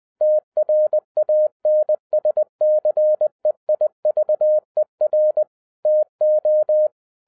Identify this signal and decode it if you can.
Morse code, heard as a keyed tone, about 20 wpm, 610 Hz. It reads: TRANSCEIVER TO
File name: 2012-01-mp3 / n05-CAME BACK TO HAM RADIO AFTER 30 YEARS OF INTERVAL-20wpm